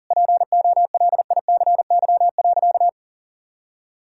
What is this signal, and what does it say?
Morse code, heard as a keyed tone, about 40 wpm, 700 Hz. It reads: POLICY.
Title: Sentences from Top 1000 Words - Review Part 5 - 40wpm